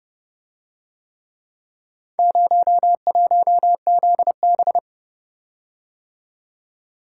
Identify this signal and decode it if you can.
Morse code, heard as a keyed tone, about 31 wpm, 700 Hz. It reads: 01ZB